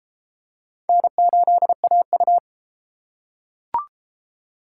Morse code, heard as a keyed tone, about 30 wpm, 700 Hz. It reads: N8AU